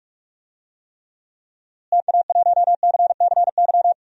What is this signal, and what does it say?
Morse code, heard as a keyed tone, about 46 wpm, 700 Hz. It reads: TA1CCY
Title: DXCC-100 Call Signs - 6 Characters - Repeated 2x - 45wpm